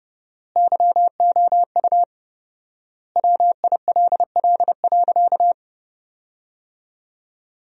Morse code, heard as a keyed tone, about 32 wpm, 700 Hz. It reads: YOU WILL.
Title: 4x Word-Spacing - Sentences from Top 500 Words - 30wpm